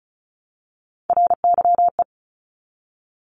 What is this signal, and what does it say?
Morse code, heard as a keyed tone, about 36 wpm, 700 Hz. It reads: RYE
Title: Multi-letter Phonemes Repeated 3x 35wpm